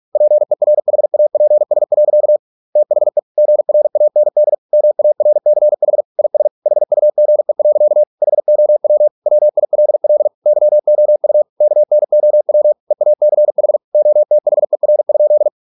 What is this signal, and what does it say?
Morse code, heard as a keyed tone, about 46 wpm, 600 Hz. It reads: PERHAPS. THE GRAND MARCH IS HUGE. HOW WILL YOU KNOW EACH OTHER?